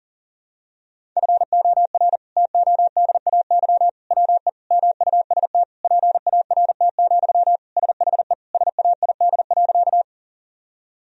Morse code, heard as a keyed tone, about 40 wpm, 700 Hz. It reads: FOR TODAY WE MUST PART, SHE SAID.
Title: Sentences from Top 1000 Words - Review Part 1 - 40wpm